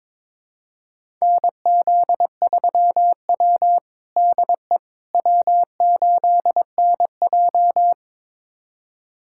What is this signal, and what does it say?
Morse code, heard as a keyed tone, about 22 wpm, 700 Hz. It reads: NZ3W DE W8NJ
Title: USA Callsign DE Callsign - 4 Characters - 22wpm